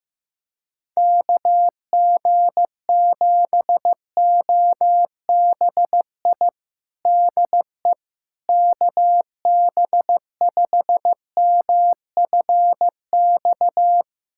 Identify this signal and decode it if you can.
Morse code, heard as a keyed tone, about 15 wpm, 700 Hz. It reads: KG7OBI DE KB5MFX
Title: USA Callsign DE Callsign - 6 Characters - 15wpm